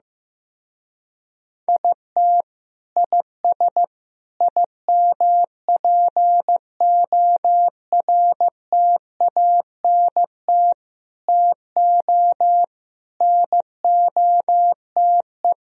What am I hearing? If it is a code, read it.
Morse code, heard as a keyed tone, about 15 wpm, 700 Hz. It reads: IT IS IMPORTANT TO NOTE
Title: Speed-Racing - Sets of 5 Words 15wpm